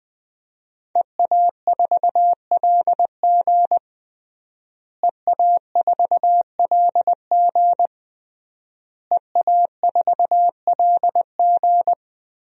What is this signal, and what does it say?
Morse code, heard as a keyed tone, about 20 wpm, 700 Hz. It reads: EA4LG EA4LG EA4LG
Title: DXCC-100 Call Signs - 5 Characters - Repeated 3x - 20wpm